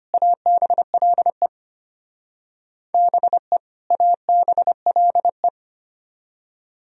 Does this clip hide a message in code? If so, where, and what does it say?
Yes: Morse code from 0.1 to 5.5 s, heard as a keyed tone, about 32 wpm, 700 Hz. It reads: ABLE BE ABLE